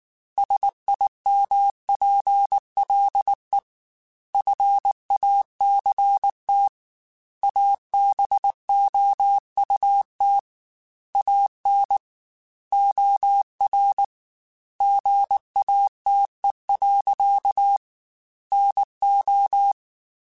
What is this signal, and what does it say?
Morse code, heard as a keyed tone, about 19 wpm, 800 Hz. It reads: SIMPLE FACT ABOUT AN OR GATE. NO
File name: Technical Subjects / Technical Subjects-mp3s / OR Gate @18